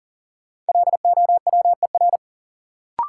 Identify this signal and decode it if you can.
Morse code, heard as a keyed tone, about 40 wpm, 700 Hz. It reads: LOWER
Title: Sentences from Top 700 Words - New Word Drill - 40wpm